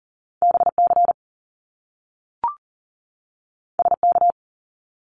Morse code, heard as a keyed tone, about 40 wpm, 700 Hz. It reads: BC SK
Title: Canadian Provinces and Territories 40wpm